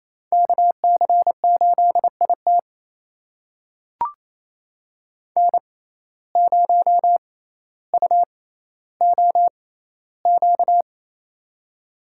Morse code, heard as a keyed tone, about 28 wpm, spaced as for 14 wpm, 700 Hz. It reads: KC8IT N0UOQ
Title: USA Call Signs - 5 Characters - 6x Character Spacing - 28wpm